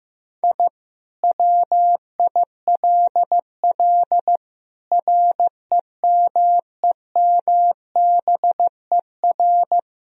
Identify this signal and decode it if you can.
Morse code, heard as a keyed tone, about 15 wpm, 700 Hz. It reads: I WILL REMEMBER
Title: Sentences from Top 1000 Words - Review Part 2 - 15wpm